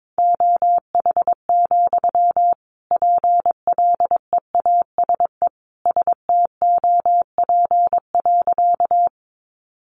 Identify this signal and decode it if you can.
Morse code, heard as a keyed tone, about 22 wpm, 700 Hz. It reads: OH, PLEASE STOP.
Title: Sentences from Top 1500 Words 22wpm